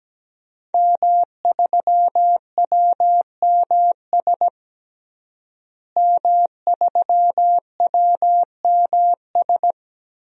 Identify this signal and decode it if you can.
Morse code, heard as a keyed tone, about 17 wpm, 700 Hz. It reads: M3WMS M3WMS
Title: DXCC-100 Call Signs - 5 Characters - Repeated 3x - 17wpm